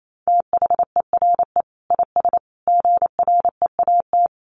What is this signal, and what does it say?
Morse code, heard as a keyed tone, about 28 wpm, 700 Hz. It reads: THERE IS GREAT
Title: Sentences from Top 1200 Words 28wpm